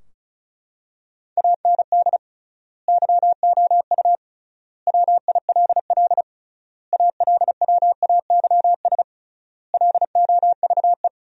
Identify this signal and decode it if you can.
Morse code, heard as a keyed tone, about 36 wpm, 700 Hz. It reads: AND YOU WILL ALWAYS LOVE
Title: 3x Word-Spacing - Sentences from Top 400 Words - 35wpm